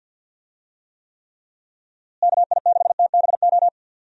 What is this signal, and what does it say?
Morse code, heard as a keyed tone, about 47 wpm, 700 Hz. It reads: KI6TBO
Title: USA Call Signs - 6 Characters - Repeated 3x - 50wpm